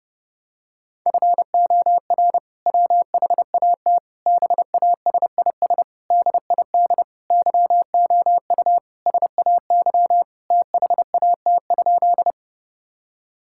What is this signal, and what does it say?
Morse code, heard as a keyed tone, about 32 wpm, 700 Hz. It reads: FOR WHAT BASIS DID YOU SAY THAT?
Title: Sentences from Top 1000 Words - Review Part 5 - 30wpm